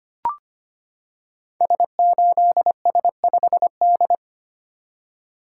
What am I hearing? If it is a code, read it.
Morse code, heard as a keyed tone, about 25 wpm, 700 Hz. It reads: S8S5D
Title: Code Groups - 5 Letter:Numbers - 25wpm